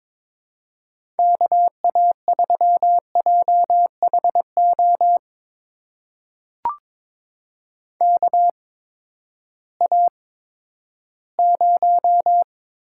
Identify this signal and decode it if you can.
Morse code, heard as a keyed tone, about 22 wpm, spaced as for 13 wpm, 700 Hz. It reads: KA3JHO KA0
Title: USA Call Signs - 6 Characters - 8x Character Spacing - 22wpm